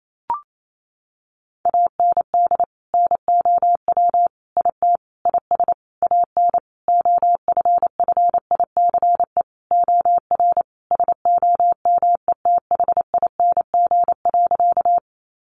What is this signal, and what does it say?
Morse code, heard as a keyed tone, about 28 wpm, 700 Hz. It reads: AND NOW IT IS AN OFFICE OR SOMETHING.
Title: Sentences from Top 1000 Words 28wpm